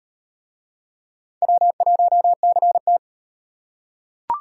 Morse code, heard as a keyed tone, about 37 wpm, 700 Hz. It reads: W1CT